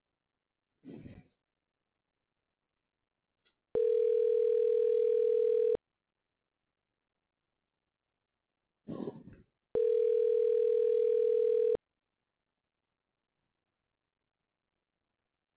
Ringback tone (North America), 440 + 480 Hz.